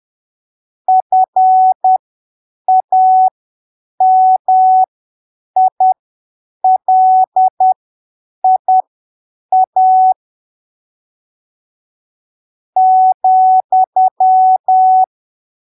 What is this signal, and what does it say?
Morse code, heard as a keyed tone, about 10 wpm, spaced as for 7 wpm, 750 Hz. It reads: FAMILIA ,
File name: arquivos / texto / texto 28 10ppm